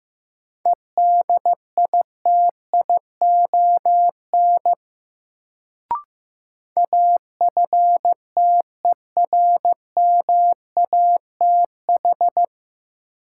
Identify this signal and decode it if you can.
Morse code, heard as a keyed tone, about 16 wpm, 700 Hz. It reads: EDITION AFTERMATH
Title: Sets of 5 Words - New Word Drill - 15wpm